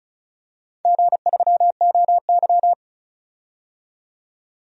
Morse code, heard as a keyed tone, about 36 wpm, 700 Hz. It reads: G3OY
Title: DXCC-100 Call Signs - 4 Characters - 35wpm